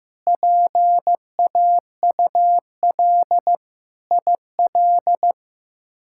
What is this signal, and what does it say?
Morse code, heard as a keyed tone, about 15 wpm, 700 Hz. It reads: PAUL IL
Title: K1USN SST QSO Practice - 15wpm